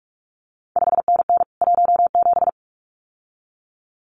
Morse code, heard as a keyed tone, about 46 wpm, 700 Hz. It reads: 5NN 17
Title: CQ WW DX Contest Exchange Encore - 45wpm